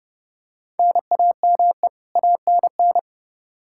Morse code, heard as a keyed tone, about 31 wpm, 700 Hz. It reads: NAME ANN
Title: K1USN SST QSO Practice 30wpm